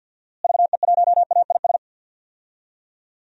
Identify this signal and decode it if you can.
Morse code, heard as a keyed tone, about 50 wpm, 700 Hz. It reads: VE1AIS
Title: DXCC-100 Call Signs - 6 Characters - Repeated 2x - 50wpm